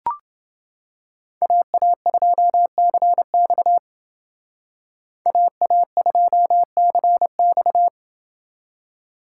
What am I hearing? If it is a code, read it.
Morse code, heard as a keyed tone, about 28 wpm, 700 Hz. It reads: AA2CX AA2CX